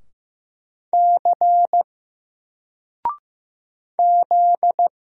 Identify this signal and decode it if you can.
Morse code, heard as a keyed tone, about 15 wpm, 700 Hz. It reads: C Z
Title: Single Character (Letter:Number:Punctuation:Prosign) 15wpm